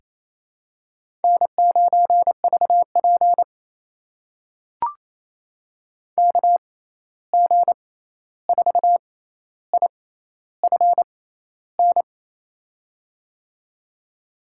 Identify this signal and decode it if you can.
Morse code, heard as a keyed tone, about 28 wpm, spaced as for 13 wpm, 700 Hz. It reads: N9VP KG4IFN
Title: USA Call Signs - 6x Character Spacing - 28wpm